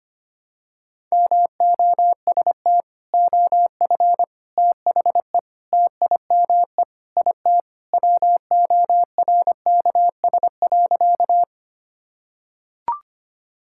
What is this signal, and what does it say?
Morse code, heard as a keyed tone, about 25 wpm, 700 Hz. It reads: MOST OF THE TIME IT WORKS.
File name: practice / 25wpm / Sentences with 3-Word Phrases - 25wpm